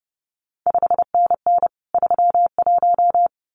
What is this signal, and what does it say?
Morse code, heard as a keyed tone, about 30 wpm, 700 Hz. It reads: 5NN 31
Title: CQ WW DX Contest Exchange - 30wpm